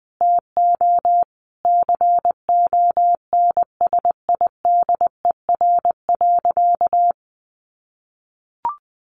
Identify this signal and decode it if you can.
Morse code, heard as a keyed tone, about 20 wpm, 700 Hz. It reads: TO CONSIDER.
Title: Sentences from Top 900 Words 20wpm